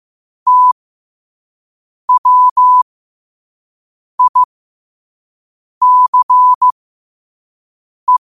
Morse code, heard as a keyed tone, about 15 wpm, spaced as for 5 wpm, 1000 Hz. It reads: TWICE
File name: qotdfiles / 5wpm